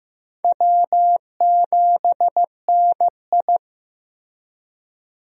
Morse code, heard as a keyed tone, about 16 wpm, 700 Hz. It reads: W7NI
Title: USA Call Signs - 4 Characters - Repeated 2x - 15wpm